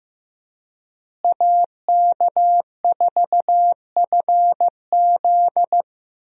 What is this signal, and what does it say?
Morse code, heard as a keyed tone, about 15 wpm, 700 Hz. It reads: AK4FZ
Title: USA Call Signs - 5 Characters - Repeated 2x - 15wpm